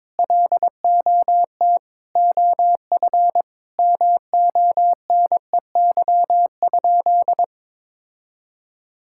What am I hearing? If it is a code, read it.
Morse code, heard as a keyed tone, about 22 wpm, 700 Hz. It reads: LOT OF MONEY?
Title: Sentences from Top 1000 Words - Review Part 5 - 22wpm